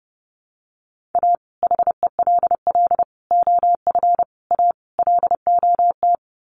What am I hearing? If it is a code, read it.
Morse code, heard as a keyed tone, about 30 wpm, 700 Hz. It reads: A HELL OF A LOT
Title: Sets of 5 Words Encore 30wpm